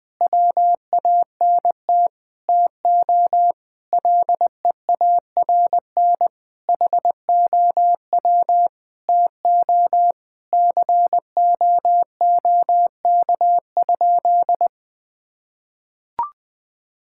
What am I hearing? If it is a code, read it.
Morse code, heard as a keyed tone, about 20 wpm, 700 Hz. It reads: WANT TO LEARN HOW TO COOK?
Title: Sentences with 6-Word Phrases - 20wpm